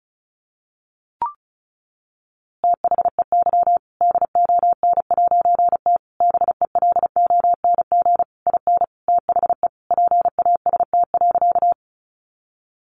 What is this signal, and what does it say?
Morse code, heard as a keyed tone, about 34 wpm, 700 Hz. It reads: THEY DON'T BELONG IN THE PAST.
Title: A Sound Of Thunder 35wpm